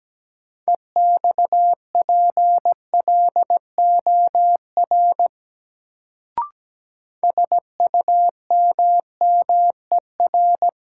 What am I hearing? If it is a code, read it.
Morse code, heard as a keyed tone, about 17 wpm, 700 Hz. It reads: EXPLOR SUMMER
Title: Sentences from Top 1100 Words - New Word Drill Alt - 17wpm